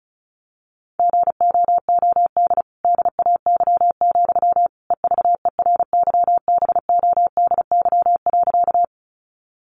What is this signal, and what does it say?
Morse code, heard as a keyed tone, about 34 wpm, 700 Hz. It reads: GOOD DAY, EVERYBODY.
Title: Sentences from Top 1400 Words 35wpm